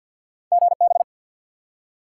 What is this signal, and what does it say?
Morse code, heard as a keyed tone, about 48 wpm, 700 Hz. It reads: GB
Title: Top 180 QSO Elements 50wpm